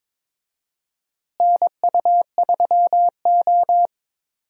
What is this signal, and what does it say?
Morse code, heard as a keyed tone, about 22 wpm, 700 Hz. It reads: NU3O
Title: USA Call Signs - 4 Characters - Repeated 3x - 22wpm